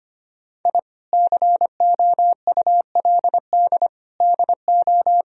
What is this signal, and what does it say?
Morse code, heard as a keyed tone, about 25 wpm, 700 Hz. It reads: I COULD DO
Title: Sentences from Top 1000 Words - Review Part 3 - 25wpm